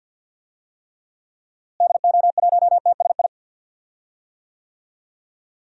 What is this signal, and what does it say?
Morse code, heard as a keyed tone, about 48 wpm, 700 Hz. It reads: DO1TSI
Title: DXCC-100 Call Signs - 6 Characters - Repeated 3x - 50wpm